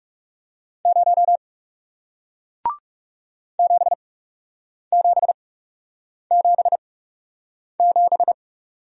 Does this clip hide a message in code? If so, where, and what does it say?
Yes: Morse code from 0.9 to 8.3 s, heard as a keyed tone, about 41 wpm, spaced as for 8 wpm, 700 Hz. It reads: 0 7777